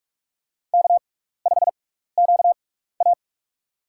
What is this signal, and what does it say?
Morse code, heard as a keyed tone, about 45 wpm, spaced as for 15 wpm, 700 Hz. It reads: K5QA
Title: USA Call Signs - 4 Characters - 6x Character Spacing - 45wpm